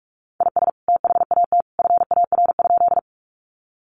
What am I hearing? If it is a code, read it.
Morse code, heard as a keyed tone, about 45 wpm, 700 Hz. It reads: IS THAT FAR?
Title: Sentences from Top 300 Words 45wpm